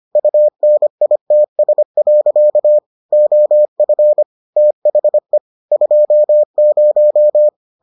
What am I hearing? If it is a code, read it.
Morse code, heard as a keyed tone, about 25 wpm, 600 Hz. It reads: UNITS. OF THE 20